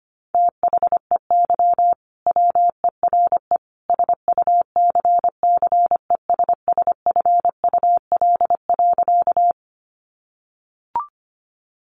Morse code, heard as a keyed tone, about 25 wpm, 700 Hz. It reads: THEY WERE SUCCESSFUL.
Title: Sentences from Top 900 Words 25wpm